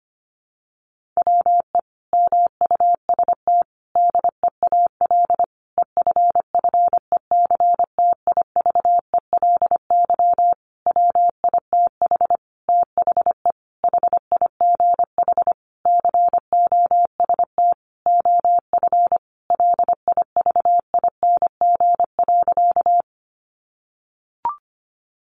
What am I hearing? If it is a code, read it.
Morse code, heard as a keyed tone, about 25 wpm, 700 Hz. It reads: WE MUST DEAL EFFECTIVELY WITH THE HIGH COST OF LIVING.
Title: Sentences from Top 1400 Words 25wpm